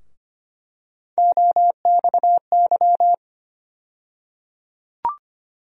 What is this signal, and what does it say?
Morse code, heard as a keyed tone, about 25 wpm, 700 Hz. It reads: OXY
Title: USA Call Sign Postfixes - 3 Letters 25wpm